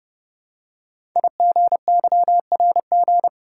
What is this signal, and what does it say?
Morse code, heard as a keyed tone, about 29 wpm, 700 Hz. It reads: IGYRG
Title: Code Groups - 5 Letters - 30wpm